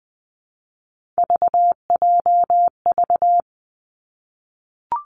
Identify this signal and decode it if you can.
Morse code, heard as a keyed tone, about 20 wpm, 700 Hz. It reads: VJV
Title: Three Challenging Letter-Number Combinations 20wpm